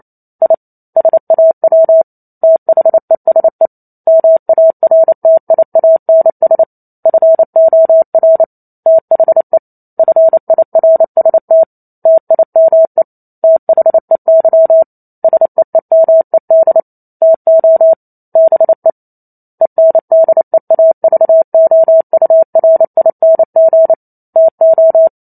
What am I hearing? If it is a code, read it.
Morse code, heard as a keyed tone, about 28 wpm, 650 Hz. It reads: I SAW THESE MARTIANS FOR THE FIRST TIME THEY SEEMED TO BE ENDEAVOURING TO
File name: Chapter 2 - 28 WPM